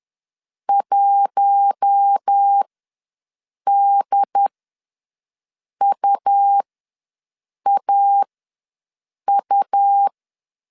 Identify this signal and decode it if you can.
Morse code, heard as a keyed tone, about 10 wpm, spaced as for 6 wpm, 800 Hz. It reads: 1DUAU
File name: LEZIONI / 6 PKX0 / 6 P ws 10wpm 3751